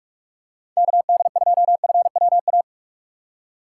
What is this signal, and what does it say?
Morse code, heard as a keyed tone, about 48 wpm, 700 Hz. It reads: KD2FWA
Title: USA Call Signs - 4x Character Spacing - 45wpm